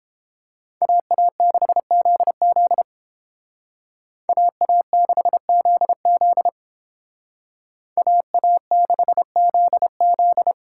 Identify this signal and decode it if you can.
Morse code, heard as a keyed tone, about 30 wpm, 700 Hz. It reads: AA6ZZ AA6ZZ AA6ZZ